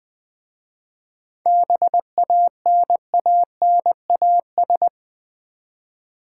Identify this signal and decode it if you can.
Morse code, heard as a keyed tone, about 20 wpm, 700 Hz. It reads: BANANAS
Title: Easy Words with Hints - 20wpm